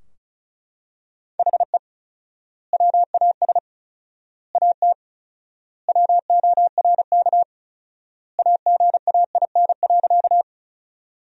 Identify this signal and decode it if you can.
Morse code, heard as a keyed tone, about 36 wpm, 700 Hz. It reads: HE WAS AT WORK AGAIN.
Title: 4x Word-Spacing - Sentences from Top 200 Words - 35wpm